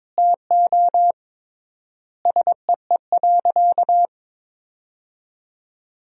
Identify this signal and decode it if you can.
Morse code, heard as a keyed tone, about 21 wpm, 700 Hz. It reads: TO SEE.
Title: 3x Word-Spacing - Sentences from Top 200 Words - 22wpm